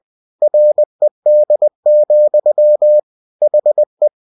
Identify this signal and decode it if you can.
Morse code, heard as a keyed tone, about 20 wpm, 600 Hz. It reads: RED, HE
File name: cw-books / winnie-the-pooh / winnie-20wpm-page-0056